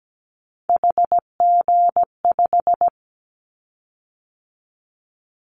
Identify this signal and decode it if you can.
Morse code, heard as a keyed tone, about 17 wpm, 700 Hz. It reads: HG5
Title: Callsign Prefixes for Top 50 DXCC - 17wpm